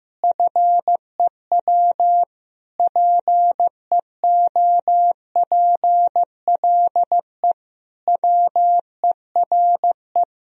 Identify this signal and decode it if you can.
Morse code, heard as a keyed tone, about 15 wpm, 700 Hz. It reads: FEW PEOPLE WERE
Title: Sentences from Top 1000 Words - Review Part 3 - 15wpm